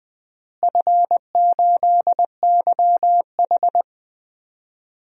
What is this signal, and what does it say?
Morse code, heard as a keyed tone, about 21 wpm, 700 Hz. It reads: F8YH
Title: DXCC-100 Call Signs - 4 Characters - Repeated 3x - 20wpm